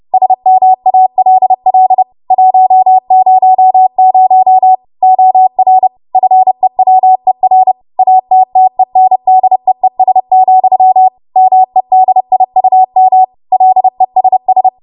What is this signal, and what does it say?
Morse code, heard as a keyed tone, about 30 wpm, 750 Hz. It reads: SMALL 100 OR FEWER ATTENDEES, MEDIUM LESS
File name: Morse / 30WPM